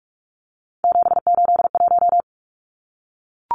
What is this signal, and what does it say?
Morse code, heard as a keyed tone, about 44 wpm, 700 Hz. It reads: 781
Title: Numbers - 3-Digit 45wpm